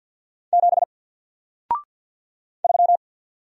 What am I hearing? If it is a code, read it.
Morse code, heard as a keyed tone, about 54 wpm, 700 Hz. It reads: 7 3
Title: Single Letter-Number Alt 50wpm